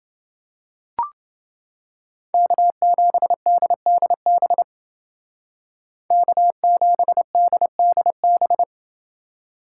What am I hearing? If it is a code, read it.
Morse code, heard as a keyed tone, about 28 wpm, 700 Hz. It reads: K7DDB K7DDB